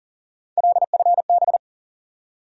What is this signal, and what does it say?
Morse code, heard as a keyed tone, about 40 wpm, 700 Hz. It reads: LFB
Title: USA Call Sign Postfixes - 3 Letters 40wpm